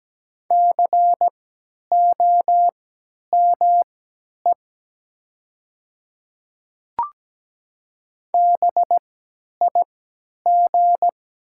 Morse code, heard as a keyed tone, about 17 wpm, spaced as for 10 wpm, 700 Hz. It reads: COME BIG
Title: 3x Character Spacing - Top 500 Words - 17wpm